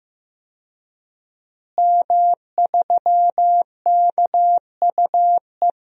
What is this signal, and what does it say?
Morse code, heard as a keyed tone, about 15 wpm, 700 Hz. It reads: M3KUE
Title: DXCC-100 Call Signs - 5 Characters - Repeated 2x - 15wpm